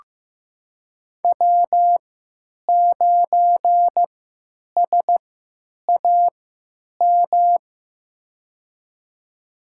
Morse code, heard as a keyed tone, about 15 wpm, spaced as for 9 wpm, 700 Hz. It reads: W9SAM
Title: USA Call Signs - 5 Characters - 3x Character Spacing - 15wpm